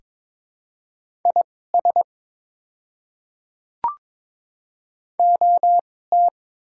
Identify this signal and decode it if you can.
Morse code, heard as a keyed tone, about 22 wpm, spaced as for 16 wpm, 700 Hz. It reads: IS OT
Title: 2x Character Spacing - Top 100 Words - 22wpm